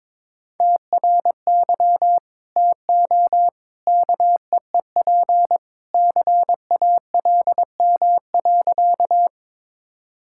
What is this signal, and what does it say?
Morse code, heard as a keyed tone, about 22 wpm, 700 Hz. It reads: TRY TO KEEP CALM.
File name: practice / 22wpm / Sentences with 2-Word Phrases - 22wpm